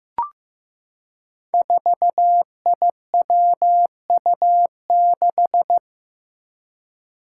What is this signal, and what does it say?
Morse code, heard as a keyed tone, about 15 wpm, 700 Hz. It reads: 4IWU6
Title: Five Challenging Letter-Number Combinations 15wpm